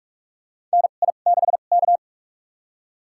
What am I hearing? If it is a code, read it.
Morse code, heard as a keyed tone, about 43 wpm, spaced as for 32 wpm, 700 Hz. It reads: NI6K
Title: USA Call Signs - 2x Character Spacing - 45wpm